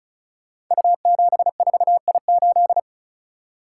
Morse code, heard as a keyed tone, about 36 wpm, 690 Hz. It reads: U74I8
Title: Code Groups - 5 Letter:Numbers - 35wpm